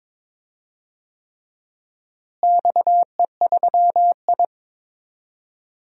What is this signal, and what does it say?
Morse code, heard as a keyed tone, about 22 wpm, 700 Hz. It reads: XE3I
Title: DXCC-100 Call Signs - 4 Characters - Repeated 3x - 22wpm